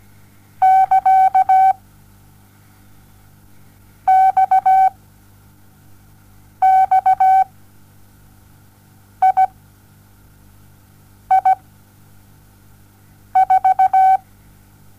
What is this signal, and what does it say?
Morse code, heard as a keyed tone, about 17 wpm, spaced as for 4 wpm, 760 Hz. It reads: <KA>XXII4
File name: Onlinelehrgang / morsen / DJ4UF Morsen07